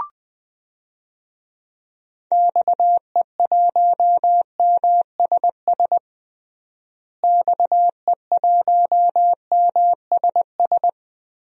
Morse code, heard as a keyed tone, about 20 wpm, 700 Hz. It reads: XE1MSS XE1MSS
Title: DXCC-100 Call Signs - 6 Characters - Repeated 3x - 20wpm